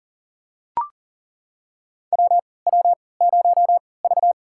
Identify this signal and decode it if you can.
Morse code, heard as a keyed tone, about 40 wpm, spaced as for 23 wpm, 700 Hz. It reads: WW0V
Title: USA Call Signs - 3x Character Spacing - 40wpm